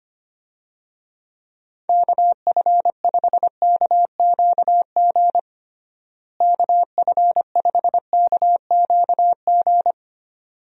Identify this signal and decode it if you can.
Morse code, heard as a keyed tone, about 25 wpm, 700 Hz. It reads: KF5KQG KF5KQG
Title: USA Call Signs - 6 Characters - Repeated 2x - 25wpm